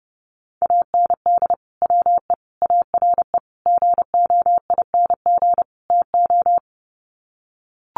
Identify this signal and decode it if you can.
Morse code, heard as a keyed tone, about 30 wpm, 700 Hz. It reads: AND WE ARE GOING TO
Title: Sets of 5 Words Encore2 30wpm